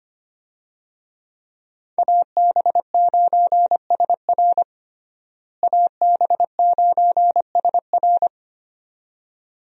Morse code, heard as a keyed tone, about 25 wpm, 700 Hz. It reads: AB9SR AB9SR
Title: USA Call Signs - 5 Characters - Repeated 2x - 25wpm